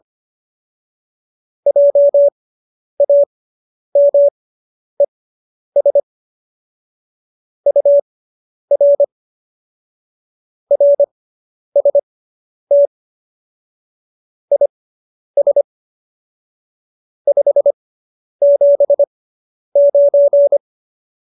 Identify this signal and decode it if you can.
Morse code, heard as a keyed tone, about 25 wpm, spaced as for 10 wpm, 570 Hz. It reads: JAMES UR RST IS 579